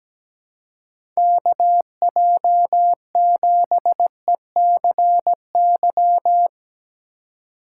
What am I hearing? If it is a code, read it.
Morse code, heard as a keyed tone, about 17 wpm, 700 Hz. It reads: KJ7ECY
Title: USA Call Signs - 6 Characters - 4x Character Spacing - 17wpm